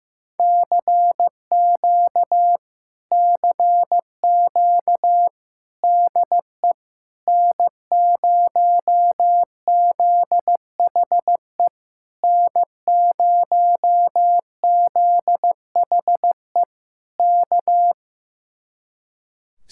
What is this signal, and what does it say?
Morse code, heard as a keyed tone, about 15 wpm, 700 Hz. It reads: CQ CQ DE N0ZHE N0ZHE K
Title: Speed-Racing Common QSO Phrases Encore - 15wpm